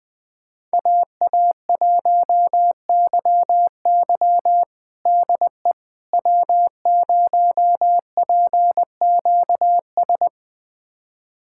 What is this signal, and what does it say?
Morse code, heard as a keyed tone, about 21 wpm, 700 Hz. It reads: AA1YY DE W0PQS
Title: USA Callsign DE Callsign - 5 Characters - 20wpm